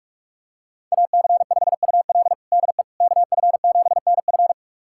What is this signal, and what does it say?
Morse code, heard as a keyed tone, about 46 wpm, 700 Hz. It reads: AC5UL DE KF7NF
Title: USA Callsign DE Callsign 45wpm